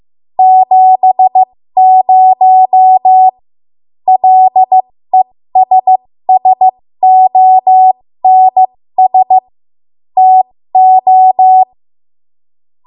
Morse code, heard as a keyed tone, about 15 wpm, spaced as for 13 wpm, 750 Hz. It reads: 70 LESSONS TO
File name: Morse / 13WPM